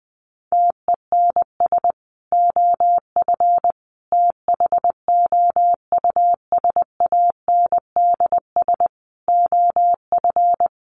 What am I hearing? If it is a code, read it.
Morse code, heard as a keyed tone, about 20 wpm, 700 Hz. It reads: TENS OF THOUSANDS OF
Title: Sets of 5 Words Encore2 20wpm